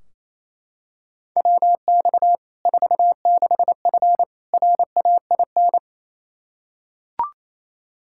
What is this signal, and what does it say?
Morse code, heard as a keyed tone, about 28 wpm, 700 Hz. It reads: WX 46F RAIN
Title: Practice QSO - Exchange 2 - Info - 28wpm